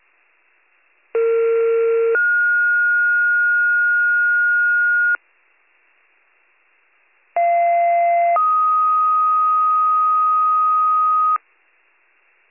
Two-tone page, heard as a one-second tone, then a three-second tone, about 460 then 1430 Hz, about 690 then 1250 Hz. It is heard from 1.1 to 11.4 s.